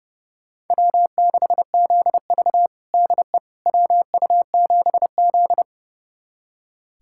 Morse code, heard as a keyed tone, about 32 wpm, 700 Hz. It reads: W6ZV DE WU7Z
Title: USA Callsign DE Callsign - 4 Characters - 30wpm